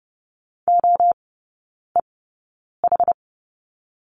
Morse code, heard as a keyed tone, about 30 wpm, spaced as for 9 wpm, 700 Hz. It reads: OEH
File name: 081 Lesson 14e - Number 2 - Three Characters 30wpm